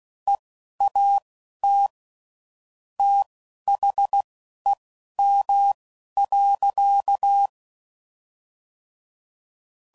Morse code, heard as a keyed tone, about 16 wpm, spaced as for 11 wpm, 800 Hz. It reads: EAT THEM.